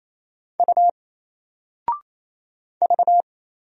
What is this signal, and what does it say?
Morse code, heard as a keyed tone, about 27 wpm, 700 Hz. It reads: U V
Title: Single Letter-Number 28wpm